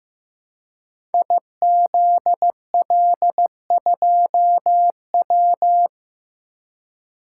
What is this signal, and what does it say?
Morse code, heard as a keyed tone, about 15 wpm, 700 Hz. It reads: IZL2W
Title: Code Groups - 5 Letter:Numbers - 15wpm